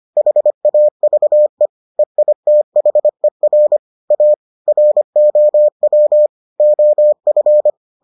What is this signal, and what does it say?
Morse code, heard as a keyed tone, about 25 wpm, 600 Hz. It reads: HAVE EITHER A ROW OF